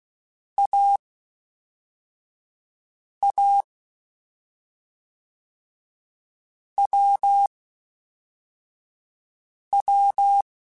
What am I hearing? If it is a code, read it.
Morse code, heard as a keyed tone, about 16 wpm, spaced as for 4 wpm, 800 Hz. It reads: AA WW
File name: Difficult Combos / Difficult Combos-mp3s / AWJ @5